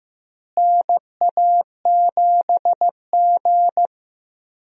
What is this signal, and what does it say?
Morse code, heard as a keyed tone, about 15 wpm, 700 Hz. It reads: NA7G
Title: USA Call Signs - 4 Characters - Repeated 3x - 15wpm